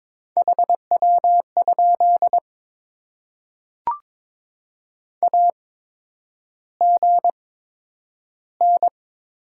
Morse code, heard as a keyed tone, about 22 wpm, spaced as for 9 wpm, 700 Hz. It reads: HW? AGN
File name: Top 20 QSO Elements - 8x Character Spacing - 22wpm